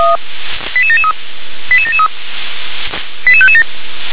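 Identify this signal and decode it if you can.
Five-tone selective call (ZVEI), heard as quick runs of five tones, tones of about 80 ms. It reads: 88773 88773 88557
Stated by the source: ZVEI five-tone call received with an SDR.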